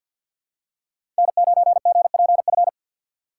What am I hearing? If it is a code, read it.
Morse code, heard as a keyed tone, about 51 wpm, 700 Hz. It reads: N9GPF
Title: USA Call Signs - 5 Characters - 4x Character Spacing - 50wpm